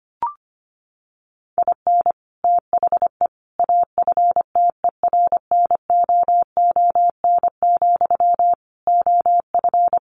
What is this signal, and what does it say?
Morse code, heard as a keyed tone, about 25 wpm, 700 Hz. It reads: IN THE AFTERNOON, OF
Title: Sentences from Top 1500 Words 25wpm